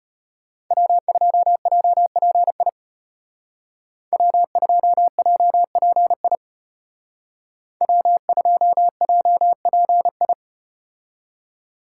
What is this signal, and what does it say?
Morse code, heard as a keyed tone, about 33 wpm, 700 Hz. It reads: W2JPI W2JPI W2JPI